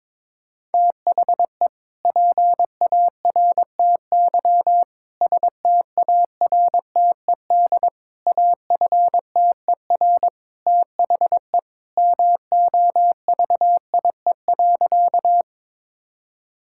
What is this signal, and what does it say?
Morse code, heard as a keyed tone, about 22 wpm, 700 Hz. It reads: THE PARTY STARTED AFTER THE MOVIE.
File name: practice / 22wpm / Sentences with 2-Word Phrases - 22wpm